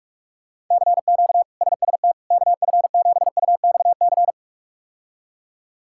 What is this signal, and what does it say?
Morse code, heard as a keyed tone, about 43 wpm, 700 Hz. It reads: CQ SST KF7UXC
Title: K1USN SST QSO Practice - 45wpm